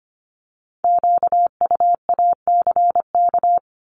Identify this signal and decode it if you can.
Morse code, heard as a keyed tone, about 25 wpm, 700 Hz. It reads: QUACK
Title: Multi-letter Phonemes 25wpm